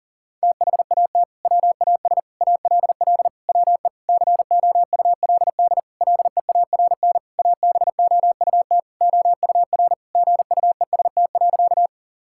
Morse code, heard as a keyed tone, about 40 wpm, 700 Hz. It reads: THAT WAS ALL WE COULD LEARN ABOUT OUR GUEST.
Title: Sentences from Top 1600 Words 40wpm